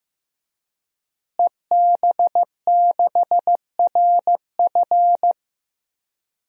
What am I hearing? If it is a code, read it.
Morse code, heard as a keyed tone, about 16 wpm, 700 Hz. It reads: EB6RF
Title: DXCC-100 Call Signs - 5 Characters - Repeated 3x - 15wpm